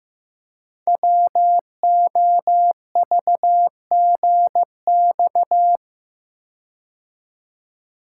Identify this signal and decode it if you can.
Morse code, heard as a keyed tone, about 15 wpm, 700 Hz. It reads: WOVGX
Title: Code Groups - 5 Letters - 15wpm